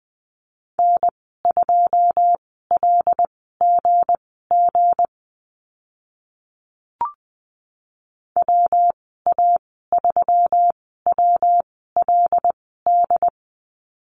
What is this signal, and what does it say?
Morse code, heard as a keyed tone, about 20 wpm, spaced as for 15 wpm, 700 Hz. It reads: N2LGG WA3WLD
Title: 193 Lesson 31g - Letter v - Call Signs 20wpm